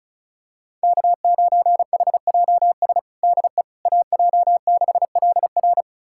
Morse code, heard as a keyed tone, about 36 wpm, 700 Hz. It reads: K9HJS DE AJ6LR
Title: USA Callsign DE Callsign - 5 Characters - 35wpm